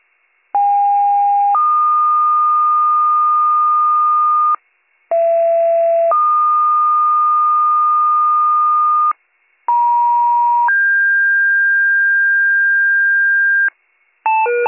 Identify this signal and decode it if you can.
Two-tone page, heard as a one-second tone, then a three-second tone, about 810 then 1220 Hz, about 670 then 1220 Hz, about 950 then 1600 Hz. It is heard from 0.5 to 13.7 s.